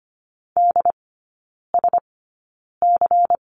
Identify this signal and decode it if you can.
Morse code, heard as a keyed tone, about 25 wpm, spaced as for 9 wpm, 700 Hz. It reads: DSC
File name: course / 25:10wpm / 093 Lesson 16e - Letter c - Three Characters 25wpm